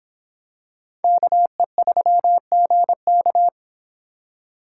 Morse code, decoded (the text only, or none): KE3GK